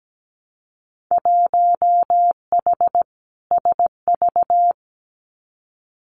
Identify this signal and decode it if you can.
Morse code, heard as a keyed tone, about 17 wpm, 700 Hz. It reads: 1H SV
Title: Winter Field Day Exchanges 17wpm